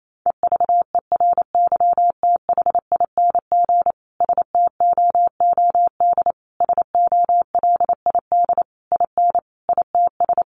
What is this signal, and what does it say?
Morse code, heard as a keyed tone, about 28 wpm, 700 Hz. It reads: EVERYTHING STOOD SOLID IN ITS
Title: Sentences from Top 1500 Words 28wpm